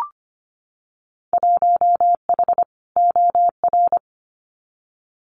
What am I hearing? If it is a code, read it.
Morse code, heard as a keyed tone, about 25 wpm, 700 Hz. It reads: 1H OR
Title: Winter Field Day Exchanges 25wpm